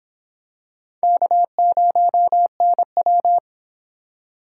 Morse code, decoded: K0NW

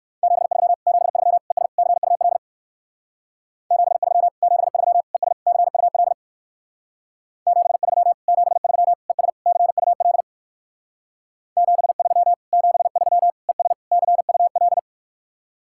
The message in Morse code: 73 73 ES CUL 73 73 ES CUL 73 73 ES CUL 73 73 ES CUL

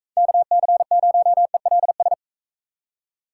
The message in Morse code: KC0ELS

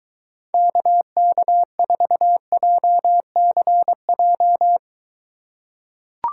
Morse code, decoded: KK4JCJ

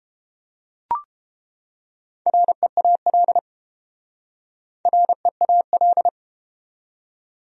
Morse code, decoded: REAL REAL

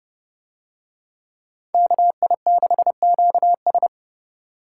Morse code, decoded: KI6QS

